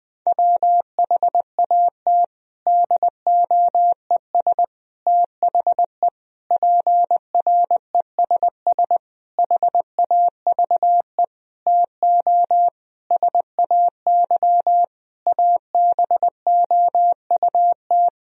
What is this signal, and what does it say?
Morse code, heard as a keyed tone, about 20 wpm, 700 Hz. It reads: WHAT DOES THE PRESS HAVE TO SAY ABOUT
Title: Sentences from Top 1000 Words - Review Part 3 - 20wpm